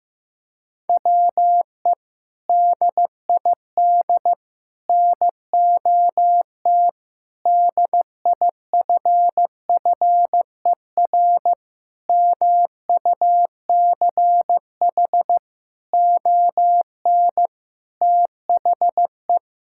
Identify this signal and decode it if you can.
Morse code, heard as a keyed tone, about 15 wpm, 700 Hz. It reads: WE DID NOT DIFFER MUCH ON THE
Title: Sentences from Top 1000 Words - Review Part 6 - 15wpm